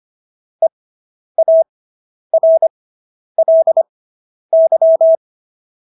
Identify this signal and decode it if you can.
Morse code, heard as a keyed tone, about 25 wpm, spaced as for 10 wpm, 650 Hz. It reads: EARLY